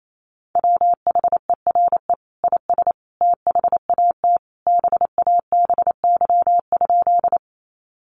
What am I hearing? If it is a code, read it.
Morse code, heard as a keyed tone, about 28 wpm, 700 Hz. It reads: WHERE IS THAT BABY?
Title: Sentences from Top 1300 Words 28wpm